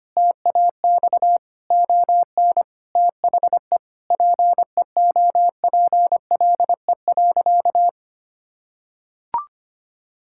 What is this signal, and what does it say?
Morse code, heard as a keyed tone, about 25 wpm, 700 Hz. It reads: TAX ON THE PEOPLE.
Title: Sentences from Top 1000 Words - Review Part 5 - 25wpm